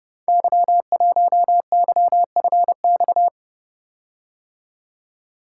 Morse code, heard as a keyed tone, about 31 wpm, 700 Hz. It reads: Y1YFX
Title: Code Groups - 5 Letter:Numbers - 30wpm